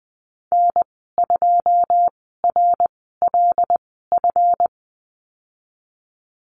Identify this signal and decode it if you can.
Morse code, heard as a keyed tone, about 20 wpm, spaced as for 15 wpm, 700 Hz. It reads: N2RLF
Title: 154 Lesson 25g - Letter p - Call Signs 20wpm